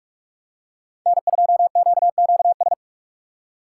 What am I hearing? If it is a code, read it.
Morse code, heard as a keyed tone, about 47 wpm, 700 Hz. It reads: N2QQS
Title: USA Call Signs - 3x Character Spacing - 45wpm